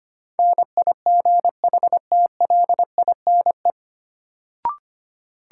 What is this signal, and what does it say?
Morse code, heard as a keyed tone, about 25 wpm, 700 Hz. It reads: NIGHTLINE
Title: Sets of 5 Words - New Word Drill - 25wpm